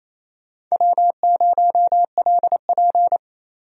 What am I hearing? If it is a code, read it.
Morse code, heard as a keyed tone, about 28 wpm, 700 Hz. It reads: W0LP
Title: USA Call Signs - 4 Characters - 28wpm